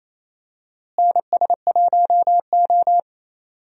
Morse code, heard as a keyed tone, about 28 wpm, 700 Hz. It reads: NS1O
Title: USA Call Signs - 4 Characters - Repeated 3x - 28wpm